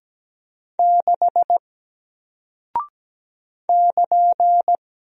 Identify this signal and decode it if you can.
Morse code, heard as a keyed tone, about 17 wpm, 700 Hz. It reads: 6 <KN>
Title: Single Character (Letter:Number:Punctuation:Prosign) Alt 17wpm